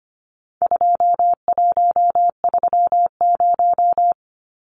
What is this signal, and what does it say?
Morse code, heard as a keyed tone, about 25 wpm, 700 Hz. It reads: 2130
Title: Numbers - 4-Digit 25wpm